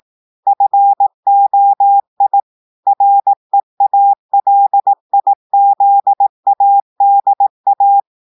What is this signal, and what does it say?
Morse code, heard as a keyed tone, about 18 wpm, 800 Hz. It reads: FOI REALIZADA
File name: audios / portugais / portugais texte en clair 1 vitesse 18 mots minutes